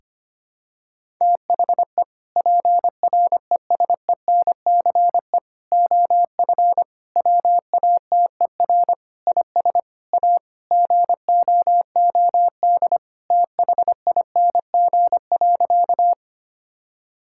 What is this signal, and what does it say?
Morse code, heard as a keyed tone, about 25 wpm, 700 Hz. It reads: THE PRESENCE OF WATER IS A GOOD THING.
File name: practice / 25wpm / Sentences with 3-Word Phrases - 25wpm